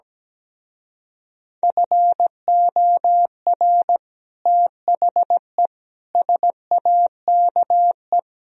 Morse code, heard as a keyed tone, about 17 wpm, 700 Hz. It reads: FOR THE SAKE
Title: Speed-Racing - Sets of 4 Words 17wpm